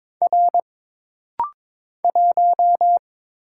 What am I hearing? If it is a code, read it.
Morse code, heard as a keyed tone, about 22 wpm, 700 Hz. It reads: R 1
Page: This rapid-fire timing is geared to push your ICR (Instant Character Recognition) skills.